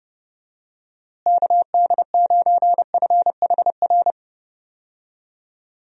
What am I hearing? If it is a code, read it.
Morse code, heard as a keyed tone, about 30 wpm, 700 Hz. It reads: KD9FHR
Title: USA Call Signs - 6 Characters - Repeated 2x - 30wpm